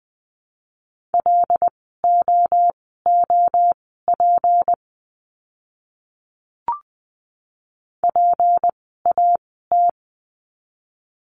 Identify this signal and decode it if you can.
Morse code, heard as a keyed tone, about 20 wpm, spaced as for 15 wpm, 700 Hz. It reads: LOOP PAT
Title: 153 Lesson 25f - Letter p - Words 20wpm